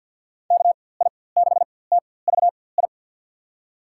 Morse code, heard as a keyed tone, about 47 wpm, spaced as for 23 wpm, 700 Hz. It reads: KI6TVI
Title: USA Call Signs - 6 Characters - 4x Character Spacing - 50wpm